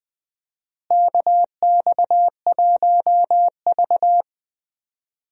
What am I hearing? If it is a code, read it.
Morse code, heard as a keyed tone, about 20 wpm, 700 Hz. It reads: KX1V
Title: USA Call Signs - 4 Characters - Repeated 3x - 20wpm